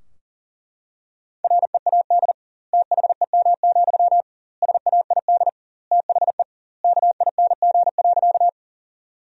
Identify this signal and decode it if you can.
Morse code, heard as a keyed tone, about 37 wpm, 700 Hz. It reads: READ THEM, SAID THE KING.
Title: 2x Word-Spacing - Sentences from Top 400 Words - 40wpm